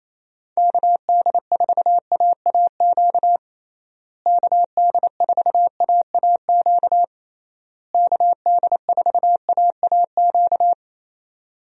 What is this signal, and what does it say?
Morse code, heard as a keyed tone, about 28 wpm, 700 Hz. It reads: KD4AAQ KD4AAQ KD4AAQ
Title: USA Call Signs - 6 Characters - Repeated 3x - 28wpm